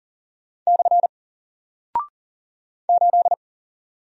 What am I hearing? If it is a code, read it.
Morse code, heard as a keyed tone, about 40 wpm, 700 Hz. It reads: / 8
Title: Single Character (Letter:Number:Punctuation:Prosign) Alt 40wpm